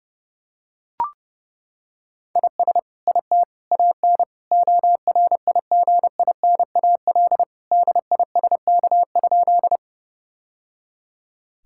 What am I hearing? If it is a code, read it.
Morse code, heard as a keyed tone, about 30 wpm, 700 Hz. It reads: IS IT AN ORIGINAL DISK?
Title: Sentences from Top 1000 Words - Review Part 6 - 30wpm